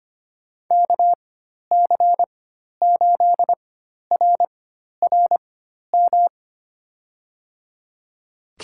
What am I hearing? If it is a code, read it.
Morse code, heard as a keyed tone, about 25 wpm, spaced as for 12 wpm, 700 Hz. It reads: KC8RRM
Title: USA Call Signs - 6 Characters - 4x Character Spacing - 25wpm